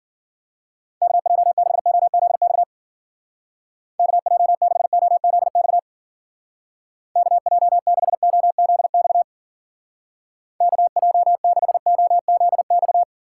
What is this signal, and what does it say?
Morse code, heard as a keyed tone, about 50 wpm, 700 Hz. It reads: KJ6OZX KJ6OZX KJ6OZX KJ6OZX
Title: Speed Racing - USA 2x3 Call Signs 40wpm